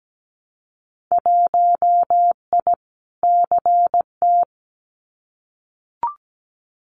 Morse code, heard as a keyed tone, about 17 wpm, 700 Hz. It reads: 1I CT
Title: Winter Field Day Exchanges 17wpm